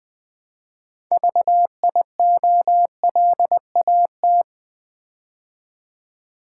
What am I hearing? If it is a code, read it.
Morse code, heard as a keyed tone, about 20 wpm, 700 Hz. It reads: VIOLAT
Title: Build A Word - 3 Letters At A Time - 9-Letter Words - 20wpm